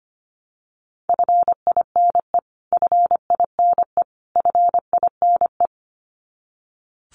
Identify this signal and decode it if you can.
Morse code, heard as a keyed tone, about 25 wpm, 700 Hz. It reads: FINE FINE FINE
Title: Top 500 Words Repeated 25wpm